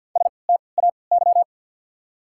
Morse code, heard as a keyed tone, about 50 wpm, spaced as for 28 wpm, 700 Hz. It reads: STAY
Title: 3x Character Spacing - Top 500 Words - 50wpm